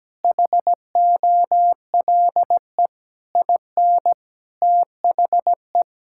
Morse code, heard as a keyed tone, about 17 wpm, 700 Hz. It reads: HOLE IN THE
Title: Sentences from Top 1600 Words 17wpm